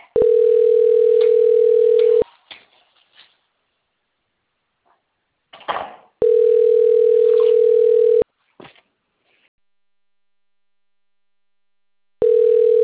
Ringback tone (North America), 440 + 480 Hz.
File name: rg-1600-unknown- - - .71285